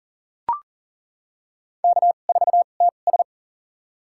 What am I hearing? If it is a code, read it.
Morse code, heard as a keyed tone, about 40 wpm, spaced as for 29 wpm, 700 Hz. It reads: K4TS
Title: USA Call Signs - 2x Character Spacing - 40wpm